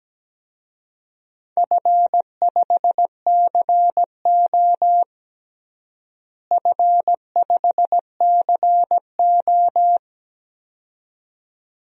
Morse code, heard as a keyed tone, about 17 wpm, 700 Hz. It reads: F5CO F5CO
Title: DXCC-100 Call Signs - 4 Characters - Repeated 2x - 17wpm